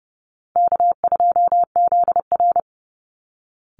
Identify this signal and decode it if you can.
Morse code, heard as a keyed tone, about 30 wpm, 700 Hz. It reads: K2ZR
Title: Call Signs Repeated 2x - 30wpm